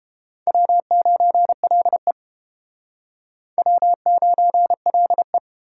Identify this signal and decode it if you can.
Morse code, heard as a keyed tone, about 32 wpm, 700 Hz. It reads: W9LE W9LE